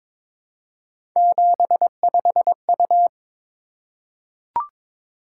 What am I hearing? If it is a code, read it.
Morse code, heard as a keyed tone, about 22 wpm, 700 Hz. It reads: 75U
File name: practice / 22wpm / Three Letter-Number Combinations - 22wpm